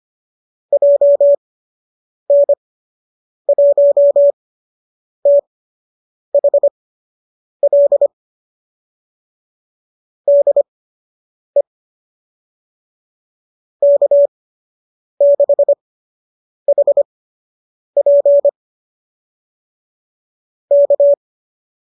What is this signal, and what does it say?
Morse code, heard as a keyed tone, about 25 wpm, spaced as for 8 wpm, 570 Hz. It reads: JN1THL DE K6HP K
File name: qso003 08